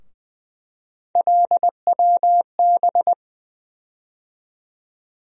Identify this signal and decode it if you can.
Morse code, heard as a keyed tone, about 20 wpm, 700 Hz. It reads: LWB
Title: USA Call Sign Postfixes - 3 Letters 20wpm